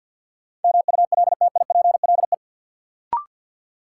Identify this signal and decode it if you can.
Morse code, heard as a keyed tone, about 52 wpm, 700 Hz. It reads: MULTIPLE
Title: Sets of 5 Words - New Word Drill - 50wpm